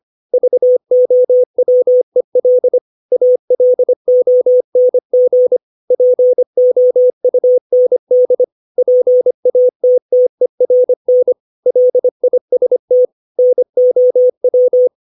Morse code, heard as a keyed tone, about 25 wpm, 500 Hz. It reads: VOWEL ALONG POUND PATTERN LIST NOW
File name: 632 words / 632 Words 25WPM